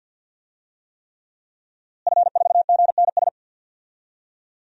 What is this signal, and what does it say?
Morse code, heard as a keyed tone, about 52 wpm, 700 Hz. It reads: F4GNS
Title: DXCC-100 Call Signs - 5 Characters - Repeated 3x - 50wpm